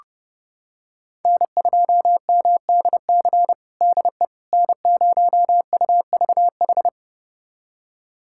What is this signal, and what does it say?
Morse code, heard as a keyed tone, about 30 wpm, 700 Hz. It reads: N2MDC DE N0UVH
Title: USA Callsign DE Callsign - 5 Characters - 30wpm